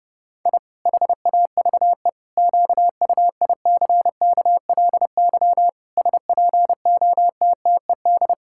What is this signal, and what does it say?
Morse code, heard as a keyed tone, about 32 wpm, 700 Hz. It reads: I HAVE QUICKLY SPOTTED
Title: Pangram Challenge v2 - 30wpm